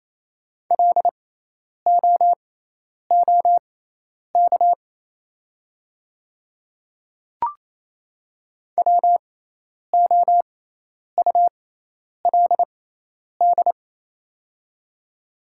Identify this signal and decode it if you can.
Morse code, heard as a keyed tone, about 28 wpm, spaced as for 10 wpm, 700 Hz. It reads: LOOK WOULD
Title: 6x Character Spacing - Top 300 Words - 28wpm